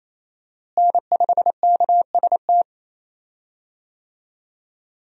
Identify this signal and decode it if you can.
Morse code, heard as a keyed tone, about 28 wpm, 700 Hz. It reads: N5KST
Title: USA Call Signs - 5 Characters - Repeated 2x - 28wpm